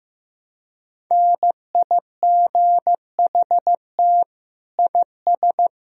Morse code, heard as a keyed tone, about 15 wpm, 700 Hz. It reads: NIGHT IS
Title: Sentences from Top 1000 Words - Review Part 1 - 15wpm